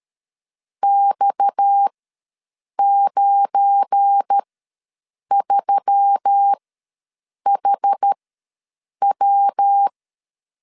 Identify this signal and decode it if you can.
Morse code, heard as a keyed tone, about 13 wpm, spaced as for 7 wpm, 800 Hz. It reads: X93HW
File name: LEZIONI / 10 SIGNS / 10A SLASH ws 12wpm 3443